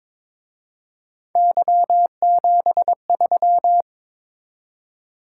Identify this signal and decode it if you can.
Morse code, heard as a keyed tone, about 26 wpm, 700 Hz. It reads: Y73
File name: Speed-Racing - Three Letter-Number Combinations - 22wpm